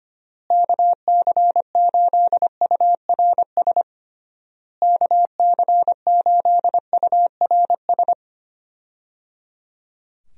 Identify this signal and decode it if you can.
Morse code, heard as a keyed tone, about 25 wpm, 700 Hz. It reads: KC8URS KC8URS
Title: USA Call Signs - 6 Characters - Repeated 3x - 25wpm